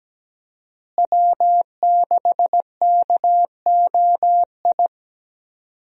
Morse code, heard as a keyed tone, about 17 wpm, 700 Hz. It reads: W6KOI
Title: USA Call Signs - 5 Characters - Repeated 3x - 17wpm